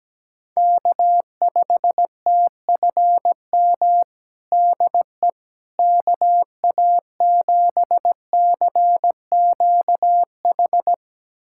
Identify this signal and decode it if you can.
Morse code, heard as a keyed tone, about 17 wpm, 700 Hz. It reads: K5TFM DE KA7CQH
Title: USA Callsign DE Callsign 17wpm